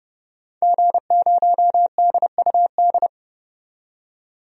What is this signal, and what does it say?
Morse code, heard as a keyed tone, about 30 wpm, 700 Hz. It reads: G0DUD
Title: DXCC-100 Call Signs - 5 Characters - 30wpm